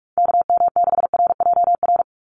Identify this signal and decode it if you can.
Morse code, heard as a keyed tone, about 46 wpm, 690 Hz. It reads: KM6RJR
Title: USA Call Signs 45wpm